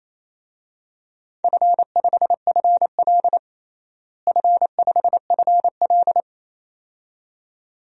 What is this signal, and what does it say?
Morse code, heard as a keyed tone, about 28 wpm, 700 Hz. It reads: F5FL F5FL
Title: DXCC-100 Call Signs - 4 Characters - Repeated 2x - 28wpm